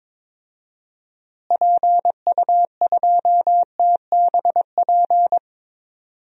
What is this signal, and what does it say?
Morse code, heard as a keyed tone, about 22 wpm, 700 Hz. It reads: PU2TBP
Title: DXCC-100 Call Signs - 6 Characters - Repeated 2x - 22wpm